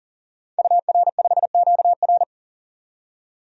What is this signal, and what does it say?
Morse code, heard as a keyed tone, about 40 wpm, 700 Hz. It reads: UR5QR
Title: DXCC-100 Call Signs - 5 Characters - 40wpm